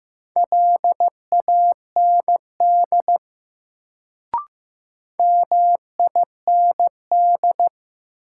Morse code, heard as a keyed tone, about 15 wpm, 700 Hz. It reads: LAND MIND
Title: Sentences from Top 500 Words - New Word Drill - 15wpm